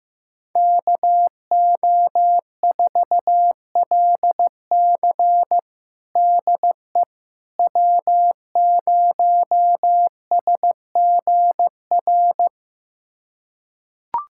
Morse code, heard as a keyed tone, about 15 wpm, 700 Hz. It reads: KO4LC DE W0SGR
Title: USA Callsign DE Callsign - 5 Characters - 15wpm